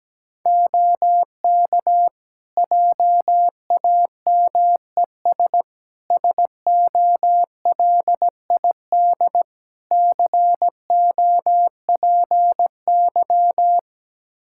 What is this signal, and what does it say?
Morse code, heard as a keyed tone, about 17 wpm, 700 Hz. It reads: OK JAMES SOLID COPY
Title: Practice QSO - Exchange 3 - More Info - 17wpm